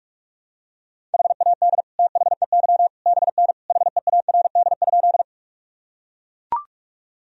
Morse code, heard as a keyed tone, about 41 wpm, 700 Hz. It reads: HAD THEY BN HEARD?
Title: Sentences from Top 400 Words and Abbreviations 45wpm